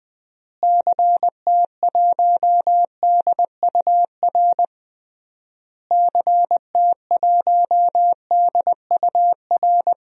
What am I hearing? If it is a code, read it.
Morse code, heard as a keyed tone, about 20 wpm, 700 Hz. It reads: CT1DUR CT1DUR
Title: DXCC-100 Call Signs - 6 Characters - Repeated 3x - 20wpm